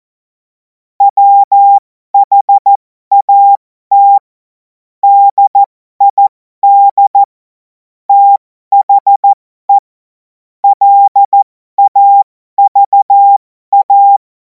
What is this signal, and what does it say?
Morse code, heard as a keyed tone, about 14 wpm, spaced as for 12 wpm, 800 Hz. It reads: WHAT DID THE LAVA
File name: Morsekurs / Witze